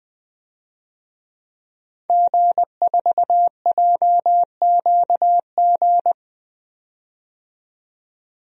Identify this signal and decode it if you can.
Morse code, heard as a keyed tone, about 20 wpm, 700 Hz. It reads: G4JQG
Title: DXCC-100 Call Signs - 5 Characters - 20wpm